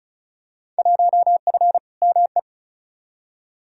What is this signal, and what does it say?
Morse code, heard as a keyed tone, about 34 wpm, 700 Hz. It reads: 1F ME
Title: Field Day Exchanges Encore - 35wpm